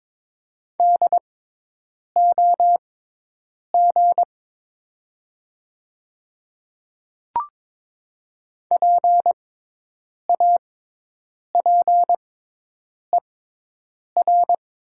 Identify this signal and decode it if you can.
Morse code, heard as a keyed tone, about 22 wpm, spaced as for 8 wpm, 700 Hz. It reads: DOG PAPER
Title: 6x Character Spacing - Top 500 Words - 22wpm